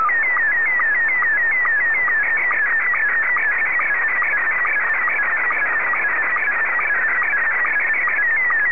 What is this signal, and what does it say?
SSTV (slow-scan television), Scottie 1.